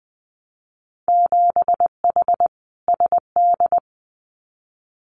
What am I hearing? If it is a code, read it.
Morse code, heard as a keyed tone, about 20 wpm, 700 Hz. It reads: 7H SD
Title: Winter Field Day Exchanges Encore 20wpm